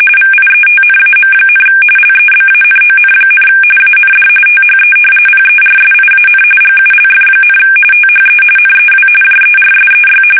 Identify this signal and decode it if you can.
RTTY, about 45.45 baud. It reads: VIETNAM, LEBANON, AND THE PERSIAN GULF, THE NEW JERSEY
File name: fsk850 / battleshipnj